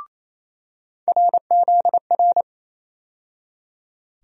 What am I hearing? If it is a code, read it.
Morse code, heard as a keyed tone, about 28 wpm, 700 Hz. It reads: RZR
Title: USA Call Sign Postfixes - 3 Letters 28wpm